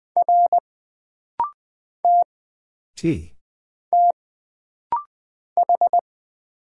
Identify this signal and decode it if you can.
Morse code, heard as a keyed tone, about 20 wpm, 700 Hz. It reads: R T T H
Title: Single Character (Letter:Number:Punctuation:Prosign) - Rapid-Fire - 20wpm